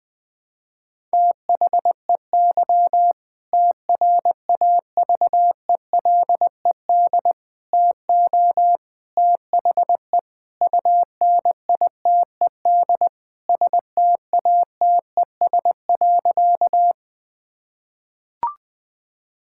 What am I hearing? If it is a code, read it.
Morse code, heard as a keyed tone, about 21 wpm, 700 Hz. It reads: THEY TRAVELED TO THE UNITED STATES.
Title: Sentences with 2-Word Phrases - Spoken Beginning - 20wpm